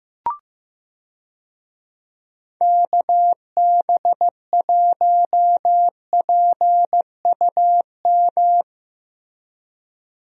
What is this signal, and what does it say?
Morse code, heard as a keyed tone, about 14 wpm, 700 Hz. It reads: KB1PUM
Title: USA Call Signs - 6 Characters - Repeated 2x - 15wpm